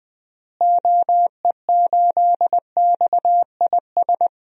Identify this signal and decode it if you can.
Morse code, heard as a keyed tone, about 21 wpm, 700 Hz. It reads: OE8XIS
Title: DXCC-100 Call Signs - 6 Characters - Repeated 3x - 20wpm